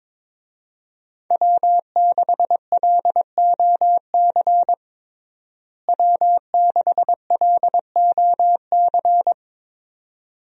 Morse code, heard as a keyed tone, about 22 wpm, 700 Hz. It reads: W6LOC W6LOC
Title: USA Call Signs - 5 Characters - Repeated 2x - 22wpm